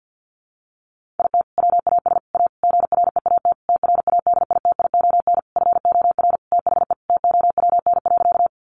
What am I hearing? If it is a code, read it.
Morse code, heard as a keyed tone, about 48 wpm, 700 Hz. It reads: IT WAS A GREAT TRADITION FOR THE TOWN.
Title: Sentences from Top 1500 Words 50wpm